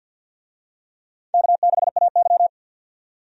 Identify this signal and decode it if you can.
Morse code, heard as a keyed tone, about 53 wpm, 700 Hz. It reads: K6AY
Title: USA Call Signs - 4 Characters - 50wpm